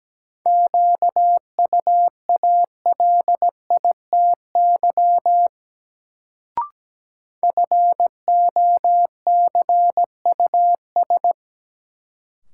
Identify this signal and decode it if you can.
Morse code, heard as a keyed tone, about 17 wpm, 700 Hz. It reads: QUALITY FOCUS
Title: Sentences from Top 1000 Words - Review Part 4 - New Word Drill - 17wpm